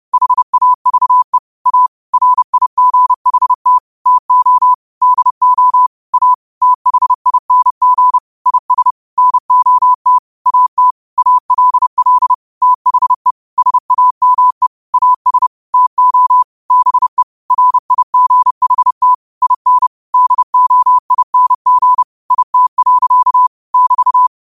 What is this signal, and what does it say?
Morse code, heard as a keyed tone, about 30 wpm, 1000 Hz. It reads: HAVE A RIGHT TO DO A THING IS NOT AT ALL THE SAME AS TO BE RIGHT IN DOING IT. <BT>
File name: 30wpm